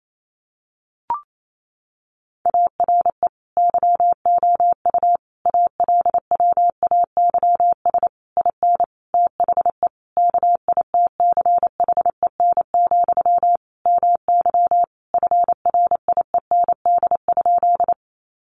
Morse code, heard as a keyed tone, about 28 wpm, 700 Hz. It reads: ARE YOU ALWAYS IN THE KITCHEN, MY FRIEND?
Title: Sentences from Top 1400 Words 28wpm